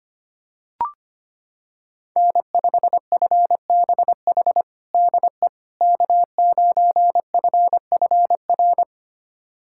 Morse code, heard as a keyed tone, about 25 wpm, 700 Hz. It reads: N5FBH DE K9FFR
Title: USA Callsign DE Callsign - 5 Characters - 25wpm